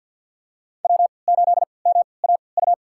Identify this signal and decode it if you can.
Morse code, heard as a keyed tone, about 52 wpm, spaced as for 29 wpm, 700 Hz. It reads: W8MAU
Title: USA Call Signs - 5 Characters - 3x Character Spacing - 50wpm